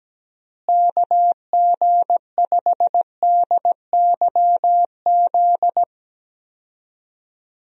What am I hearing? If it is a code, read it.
Morse code, heard as a keyed tone, about 17 wpm, 700 Hz. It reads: KG5DYZ
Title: USA Call Signs - 6 Characters - Repeated 2x - 17wpm